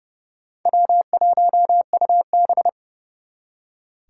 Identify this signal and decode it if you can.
Morse code, heard as a keyed tone, about 38 wpm, 700 Hz. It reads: W1UB
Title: Speed Racing - USA Call Signs 30wpm